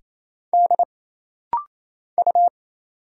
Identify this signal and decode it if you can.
Morse code, heard as a keyed tone, about 27 wpm, 700 Hz. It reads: D U
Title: Single Letters - Rapid-Fire 28wpm